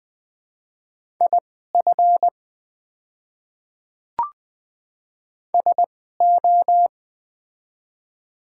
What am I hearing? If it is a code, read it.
Morse code, heard as a keyed tone, about 20 wpm, spaced as for 14 wpm, 700 Hz. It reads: IF SO
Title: 2x Character Spacing - Top 200 Words - 20wpm